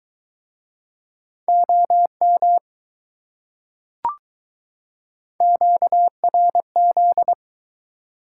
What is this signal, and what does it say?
Morse code, heard as a keyed tone, about 21 wpm, 700 Hz. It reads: OM QRZ